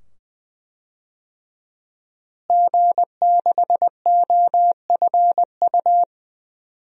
Morse code, heard as a keyed tone, about 20 wpm, 700 Hz. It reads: G6OFU
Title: DXCC-100 Call Signs - 5 Characters - Repeated 2x - 20wpm